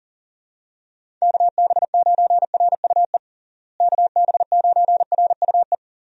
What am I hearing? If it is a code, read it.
Morse code, heard as a keyed tone, about 37 wpm, 700 Hz. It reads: KB9RUE KB9RUE
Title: USA Call Signs - 6 Characters - Repeated 3x - 40wpm